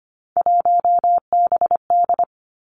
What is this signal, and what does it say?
Morse code, heard as a keyed tone, about 25 wpm, 700 Hz. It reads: 1BD
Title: Three Challenging Letter-Number Combinations 25wpm